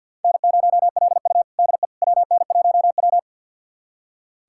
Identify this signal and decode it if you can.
Morse code, heard as a keyed tone, about 46 wpm, 700 Hz. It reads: N0LU DE WN1W
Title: USA Callsign DE Callsign - 4 Characters - 50wpm